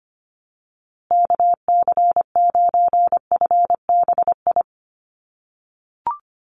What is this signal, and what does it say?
Morse code, heard as a keyed tone, about 25 wpm, 700 Hz. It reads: KC9FBI
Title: USA Call Signs Repeated 2x 25wpm